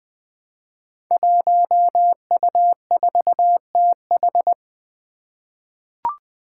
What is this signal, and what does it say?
Morse code, heard as a keyed tone, about 20 wpm, 700 Hz. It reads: 1U4TH
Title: Five Challenging Letter-Number Combinations 20wpm